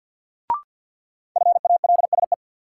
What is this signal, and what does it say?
Morse code, heard as a keyed tone, about 53 wpm, 700 Hz. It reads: FALSE
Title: Sentences from Top 1600 Words - New Word Drill - 50wpm